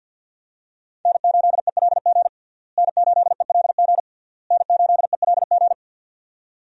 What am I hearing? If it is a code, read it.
Morse code, heard as a keyed tone, about 51 wpm, 700 Hz. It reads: N8ELG N8ELG N8ELG
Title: USA Call Signs - 5 Characters - Repeated 3x - 50wpm